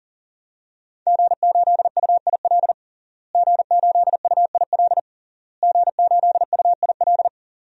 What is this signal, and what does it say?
Morse code, heard as a keyed tone, about 40 wpm, 700 Hz. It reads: G8UIL G8UIL G8UIL
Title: DXCC-100 Call Signs - 5 Characters - Repeated 3x - 40wpm